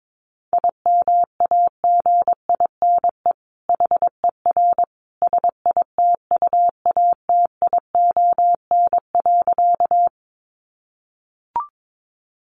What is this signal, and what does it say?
Morse code, heard as a keyed tone, about 22 wpm, 700 Hz. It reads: IMAGINE HER SITUATION.
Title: Sentences from Top 1400 Words 22wpm